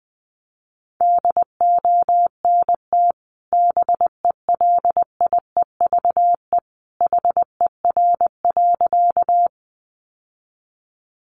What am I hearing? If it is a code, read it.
Morse code, heard as a keyed tone, about 20 wpm, 700 Hz. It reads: DONT BELIEVE HER.
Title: Sentences from Top 1000 Words 20wpm